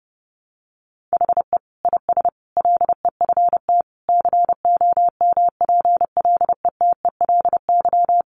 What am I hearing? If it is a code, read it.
Morse code, heard as a keyed tone, about 30 wpm, 700 Hz. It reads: HE IS LEFT COMPLETELY
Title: Sentences from Top 1100 Words 30wpm